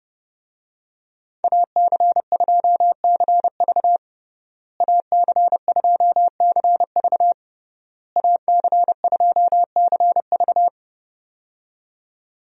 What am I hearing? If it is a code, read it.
Morse code, heard as a keyed tone, about 32 wpm, 700 Hz. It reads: AC2CV AC2CV AC2CV
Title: USA Call Signs - 5 Characters - Repeated 3x - 30wpm